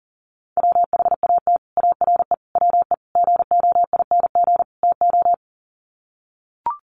Morse code, heard as a keyed tone, about 40 wpm, 700 Hz. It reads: WHAT ARE WE GOING TO
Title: Sets of 5 Words 40wpm